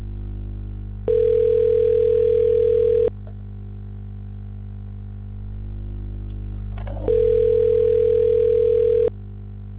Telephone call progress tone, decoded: ringback tone

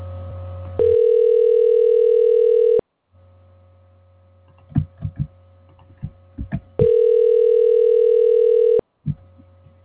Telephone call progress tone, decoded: ringback tone

